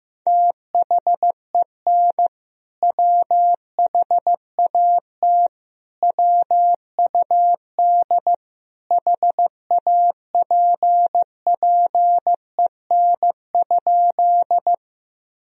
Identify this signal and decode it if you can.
Morse code, heard as a keyed tone, about 15 wpm, 700 Hz. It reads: THEN WHAT WUD HAPPEN?
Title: Sentences from Top 400 Words and Abbreviations 15wpm